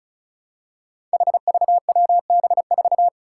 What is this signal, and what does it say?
Morse code, heard as a keyed tone, about 36 wpm, 700 Hz. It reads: HVWB4
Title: Five Challenging Letter-Number Combinations 35wpm